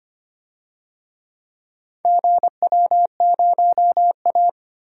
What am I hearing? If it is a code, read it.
Morse code, heard as a keyed tone, about 25 wpm, 700 Hz. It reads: GW0A
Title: DXCC-100 Call Signs - 4 Characters - Repeated 2x - 25wpm